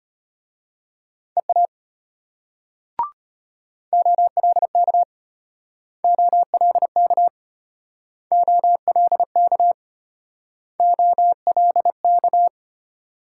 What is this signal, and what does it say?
Morse code, heard as a keyed tone, about 34 wpm, 700 Hz. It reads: EA OLK OLK OLK OLK